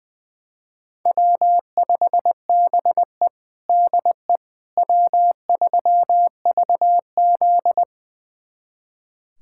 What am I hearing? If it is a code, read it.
Morse code, heard as a keyed tone, about 20 wpm, 700 Hz. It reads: W5BE DE W3VZ
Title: USA Callsign DE Callsign - 4 Characters - 20wpm